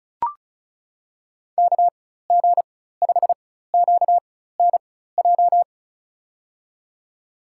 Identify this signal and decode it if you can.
Morse code, heard as a keyed tone, about 36 wpm, spaced as for 16 wpm, 700 Hz. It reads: KG5QNJ
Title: USA Call Signs - 6 Characters - 4x Character Spacing - 35wpm